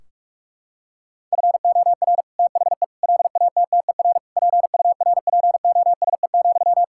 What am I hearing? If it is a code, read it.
Morse code, heard as a keyed tone, about 41 wpm, 700 Hz. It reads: FOR THE LATTER PURPOSE,
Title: Sentences from Top 1600 Words 45wpm